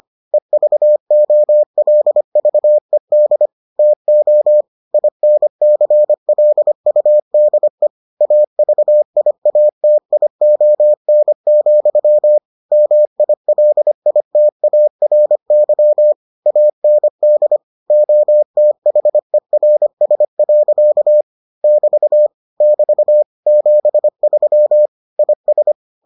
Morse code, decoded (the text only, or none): EVOLVED TO INCLUDE AVIATION, MILITARY AND OTHERS. <BT> <BT> 73 IS